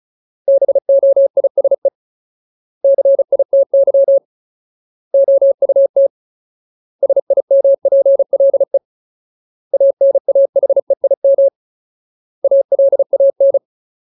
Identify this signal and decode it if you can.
Morse code, heard as a keyed tone, about 34 wpm, 550 Hz. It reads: BOISE CITY OUT SIMPLE ANAHEIM ALAN
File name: Ultimate Word List / 35WPM Ultimate